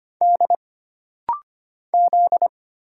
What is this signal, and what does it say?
Morse code, heard as a keyed tone, about 25 wpm, 700 Hz. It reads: D Z
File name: Single Character (Letter:Number:Punctuation:Prosign) - Rapid-Fire - 25wpm